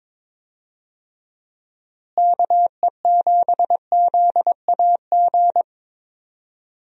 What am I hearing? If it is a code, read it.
Morse code, heard as a keyed tone, about 22 wpm, 700 Hz. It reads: KE7ZAG
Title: USA Call Signs - 6 Characters - Repeated 3x - 22wpm